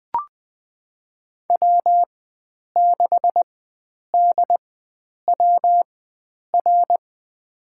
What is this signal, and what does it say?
Morse code, heard as a keyed tone, about 20 wpm, spaced as for 9 wpm, 700 Hz. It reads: W6DWR
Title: USA Call Signs - 5 Characters - 4x Character Spacing - 20wpm